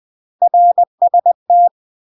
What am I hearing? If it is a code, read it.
Morse code, heard as a keyed tone, about 20 wpm, 700 Hz. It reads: RST
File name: UK MASTER QSOs TRAINING / 20WPM / UK MASTER QSOs 0014